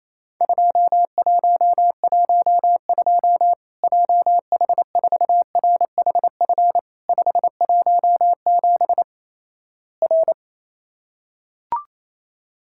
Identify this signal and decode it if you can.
Morse code, heard as a keyed tone, about 28 wpm, 700 Hz. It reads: 2112 JH4RHF 517 R
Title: Worked All Europe DX Contest - QTC Exchange - 28wpm